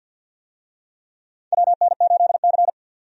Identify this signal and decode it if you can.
Morse code, heard as a keyed tone, about 52 wpm, 700 Hz. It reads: WN8C
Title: USA Call Signs - 4 Characters - 50wpm